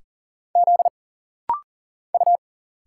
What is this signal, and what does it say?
Morse code, heard as a keyed tone, about 40 wpm, 700 Hz. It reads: Z U
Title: Single Letters - Rapid-Fire 40wpm